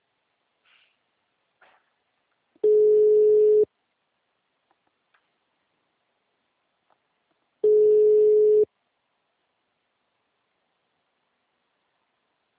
Ringback tone (Europe, 425 Hz).